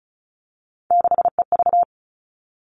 Morse code, heard as a keyed tone, about 34 wpm, 700 Hz. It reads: 6EV
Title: All Cut Numbers - 3-Digit 35wpm